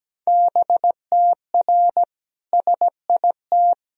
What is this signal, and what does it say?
Morse code, heard as a keyed tone, about 17 wpm, 700 Hz. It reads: BTR SIT
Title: Sentences from Top 500 Words and Abbreviations 17wpm